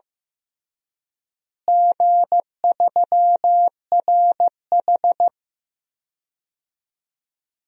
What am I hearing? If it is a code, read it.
Morse code, heard as a keyed tone, about 15 wpm, 700 Hz. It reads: G3RH
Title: DXCC-100 Call Signs - 4 Characters - Repeated 3x - 15wpm